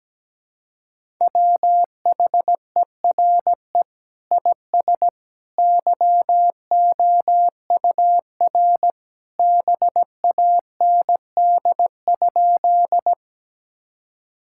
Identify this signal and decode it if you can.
Morse code, heard as a keyed tone, about 17 wpm, 700 Hz. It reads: WHERE IS YOUR BAND?
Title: Sentences from Top 1600 Words 17wpm